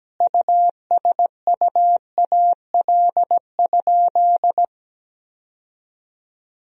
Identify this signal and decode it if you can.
Morse code, heard as a keyed tone, about 17 wpm, 700 Hz. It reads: USUAL?
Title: Sentences from Top 1000 Words - Review Part 2 - 17wpm